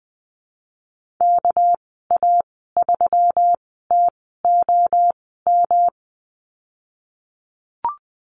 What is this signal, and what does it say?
Morse code, heard as a keyed tone, about 20 wpm, spaced as for 14 wpm, 700 Hz. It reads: KA3TOM
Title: 213 Lesson 34g - Number 8 - Call Signs 20wpm